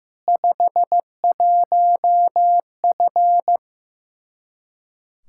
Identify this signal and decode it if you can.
Morse code, heard as a keyed tone, about 15 wpm, 700 Hz. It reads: 51F
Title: Practice QSO - Exchange 2 - Info - 15wpm